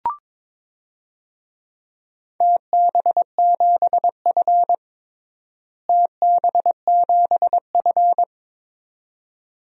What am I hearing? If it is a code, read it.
Morse code, heard as a keyed tone, about 21 wpm, 700 Hz. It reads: TB7F TB7F
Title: DXCC-100 Call Signs - 4 Characters - Repeated 2x - 22wpm